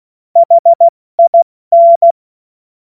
Morse code, heard as a keyed tone, about 16 wpm, 680 Hz. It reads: HIN
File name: Morsekurs / Witze / Witz 13. Januar 2026 70 F80